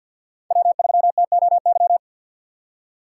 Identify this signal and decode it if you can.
Morse code, heard as a keyed tone, about 47 wpm, 700 Hz. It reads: W3TOY
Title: USA Call Signs - 4x Character Spacing - 50wpm